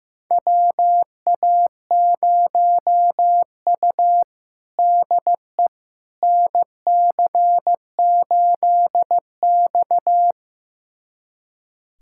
Morse code, heard as a keyed tone, about 16 wpm, 700 Hz. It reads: WA0U DE NC8X
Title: USA Callsign DE Callsign - 4 Characters - 15wpm